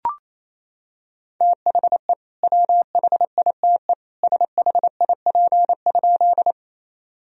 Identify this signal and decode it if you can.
Morse code, heard as a keyed tone, about 27 wpm, 700 Hz. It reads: THE WHITE SHIP?
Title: Sentences from Top 1000 Words - Review Part 2 - 28wpm